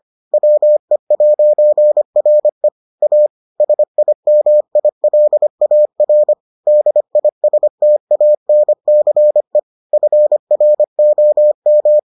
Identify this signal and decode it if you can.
Morse code, heard as a keyed tone, about 25 wpm, 600 Hz. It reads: WE'RE A SIMILAR DISTANCE FROM